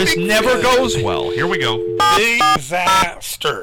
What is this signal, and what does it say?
DTMF, heard as telephone touch tones, tones of about 140 ms, with speech. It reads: 740